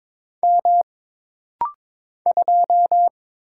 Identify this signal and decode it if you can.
Morse code, heard as a keyed tone, about 23 wpm, 700 Hz. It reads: M 2
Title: Single Letter-Number Alt - Rapid-Fire 22wpm